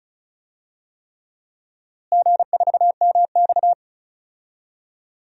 Morse code, heard as a keyed tone, about 36 wpm, 700 Hz. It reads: G4MX
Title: DXCC-100 Call Signs - 4 Characters - Repeated 3x - 35wpm